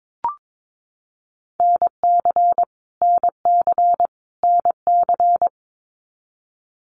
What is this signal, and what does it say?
Morse code, heard as a keyed tone, about 22 wpm, 700 Hz. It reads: NC NC NC
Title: US States Repeated 3x 22wpm